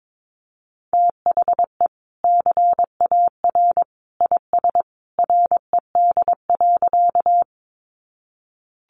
Morse code, heard as a keyed tone, about 22 wpm, 700 Hz. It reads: THE CAR IS RED.
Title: Easy Phrases 22wpm